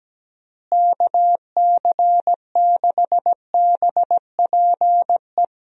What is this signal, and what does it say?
Morse code, heard as a keyed tone, about 17 wpm, 700 Hz. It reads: KC6BPE
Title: USA Call Signs - 6 Characters - Repeated 2x - 17wpm